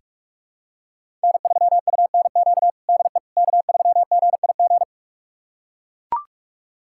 Morse code, heard as a keyed tone, about 41 wpm, 700 Hz. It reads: N3UNQ DE K3GIG
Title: USA Callsign DE Callsign - 5 Characters - 45wpm